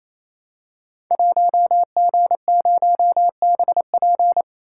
Morse code, heard as a keyed tone, about 29 wpm, 700 Hz. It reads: 1G0BP
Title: Code Groups - 5 Letter:Numbers - 28wpm